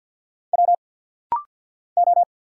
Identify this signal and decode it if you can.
Morse code, heard as a keyed tone, about 50 wpm, 700 Hz. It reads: W O
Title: Single Letters - Rapid-Fire 50wpm